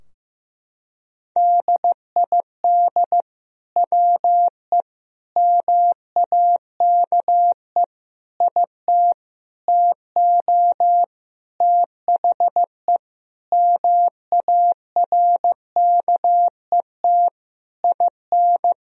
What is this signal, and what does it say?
Morse code, heard as a keyed tone, about 15 wpm, 700 Hz. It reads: DID WE MAKE IT TO THE MARKET IN
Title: Sentences from Top 1000 Words - Review Part 3 - 15wpm